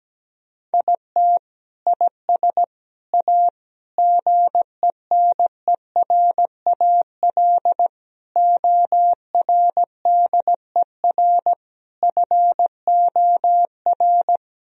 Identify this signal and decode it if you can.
Morse code, heard as a keyed tone, about 17 wpm, 700 Hz. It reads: IT IS A GENERAL ORDER FOR
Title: Sentences from Top 1000 Words - Review Part 3 - 17wpm